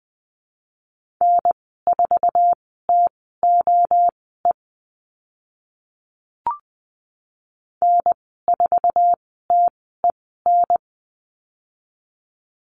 Morse code, heard as a keyed tone, about 20 wpm, spaced as for 14 wpm, 700 Hz. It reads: N4TOE N4TEN
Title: 062 Lesson 11g - Letter h - Call Signs 20wpm